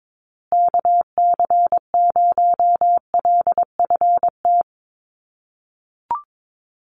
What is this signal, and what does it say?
Morse code, heard as a keyed tone, about 22 wpm, 700 Hz. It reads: KC0LFT
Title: USA Call Signs Repeated 2x 22wpm